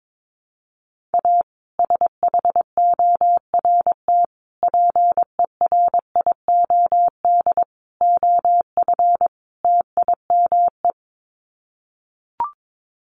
Morse code, heard as a keyed tone, about 22 wpm, 700 Hz. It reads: A SHORT PERIOD OF TIME
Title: Sets of 5 Words Encore 22wpm